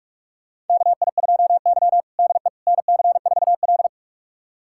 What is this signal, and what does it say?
Morse code, heard as a keyed tone, about 42 wpm, 700 Hz. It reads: KI2Y DE NC4L
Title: USA Callsign DE Callsign - 4 Characters - 45wpm